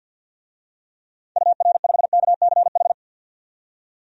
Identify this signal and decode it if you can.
Morse code, heard as a keyed tone, about 52 wpm, 700 Hz. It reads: UR5KCH
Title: DXCC-100 Call Signs - 6 Characters - 50wpm